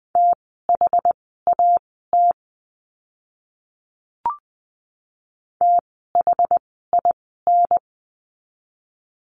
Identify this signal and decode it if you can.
Morse code, heard as a keyed tone, about 20 wpm, spaced as for 14 wpm, 700 Hz. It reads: THAT THIN
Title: 068 Lesson 12f - Letter d - Words 20wpm